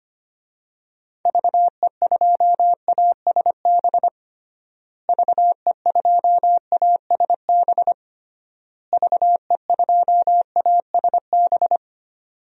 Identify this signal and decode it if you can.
Morse code, heard as a keyed tone, about 25 wpm, 700 Hz. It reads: VE2ASB VE2ASB VE2ASB
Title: DXCC-100 Call Signs - 6 Characters - Repeated 3x - 25wpm